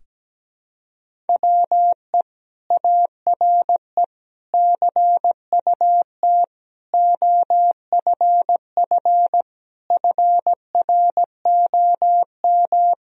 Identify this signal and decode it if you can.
Morse code, heard as a keyed tone, about 17 wpm, 700 Hz. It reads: WE ARE CUT OFF FROM
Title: Sentences from Top 1000 Words - Review Part 1 - 17wpm